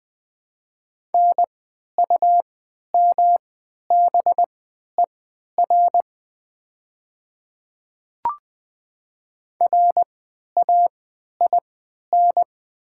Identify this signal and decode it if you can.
Morse code, heard as a keyed tone, about 20 wpm, spaced as for 11 wpm, 700 Hz. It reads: NUMBER RAIN
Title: 3x Character Spacing - Top 300 Words - 20wpm